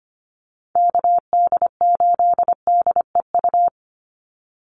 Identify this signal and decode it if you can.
Morse code, heard as a keyed tone, about 25 wpm, 700 Hz. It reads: KD8DEU
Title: NA SOTA Chaser Callsigns 25wpm